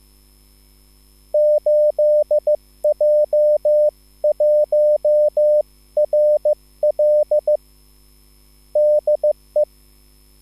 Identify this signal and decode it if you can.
Morse code, heard as a keyed tone, about 15 wpm, 600 Hz. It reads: 8J1RL DE
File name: QSL / CW lesson